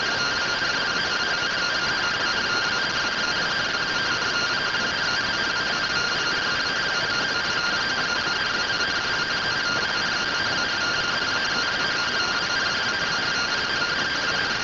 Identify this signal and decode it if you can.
Two-tone teleprinter signal, about 170 Hz shift.